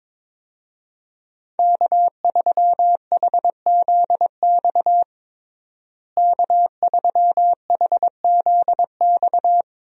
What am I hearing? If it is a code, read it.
Morse code, heard as a keyed tone, about 22 wpm, 700 Hz. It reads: K3HZX K3HZX
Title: USA Call Signs - 5 Characters - Repeated 2x - 22wpm